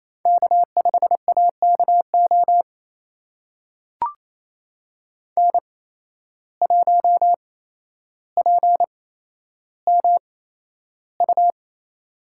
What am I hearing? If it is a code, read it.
Morse code, heard as a keyed tone, about 28 wpm, spaced as for 12 wpm, 700 Hz. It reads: K5AKO N1PMU
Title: USA Call Signs - 5 Characters - 8x Character Spacing - 28wpm